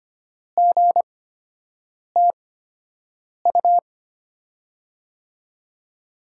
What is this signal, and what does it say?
Morse code, heard as a keyed tone, about 25 wpm, spaced as for 7 wpm, 700 Hz. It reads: GTU
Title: USA Call Signs - 8x Character Spacing - 25wpm